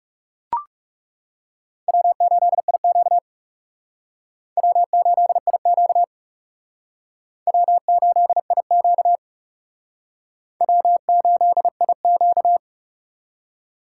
Morse code, heard as a keyed tone, about 36 wpm, 700 Hz. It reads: W8IQ W8IQ W8IQ W8IQ
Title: Speed Racing - USA 1x2 Call Signs 30wpm